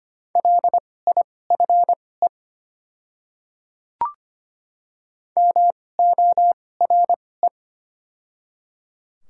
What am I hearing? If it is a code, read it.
Morse code, heard as a keyed tone, about 25 wpm, spaced as for 18 wpm, 700 Hz. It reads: LIFE MORE
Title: 2x Character Spacing - Top 300 Words - 25wpm